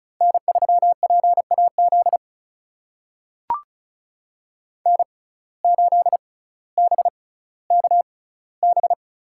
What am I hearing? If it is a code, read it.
Morse code, heard as a keyed tone, about 36 wpm, spaced as for 18 wpm, 700 Hz. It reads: N3PAZ N8BKB
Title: USA Call Signs - 5 Characters - 6x Character Spacing - 35wpm